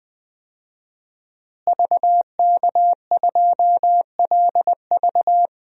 Morse code, heard as a keyed tone, about 20 wpm, 700 Hz. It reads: VK2LV
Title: DXCC-100 Call Signs - 5 Characters - Repeated 2x - 20wpm